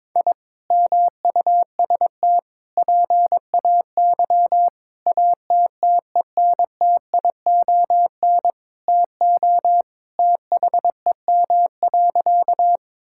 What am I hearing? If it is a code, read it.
Morse code, heard as a keyed tone, about 22 wpm, 700 Hz. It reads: I MUST PAY ATTENTION TO THEM.
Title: Sentences from Top 1000 Words - Review Part 6 - 22wpm